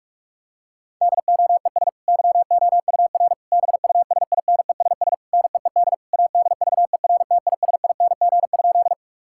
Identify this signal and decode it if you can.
Morse code, heard as a keyed tone, about 47 wpm, 700 Hz. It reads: DOES YOUR BUSINESS NEED ADVERTISING?
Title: Sentences from Top 1000 Words - Review Part 6 - 45wpm